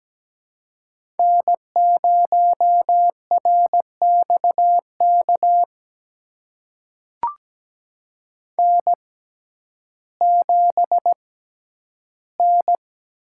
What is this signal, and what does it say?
Morse code, heard as a keyed tone, about 17 wpm, spaced as for 10 wpm, 700 Hz. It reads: N0RXK N7N
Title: USA Call Signs - 5 Characters - 6x Character Spacing - 17wpm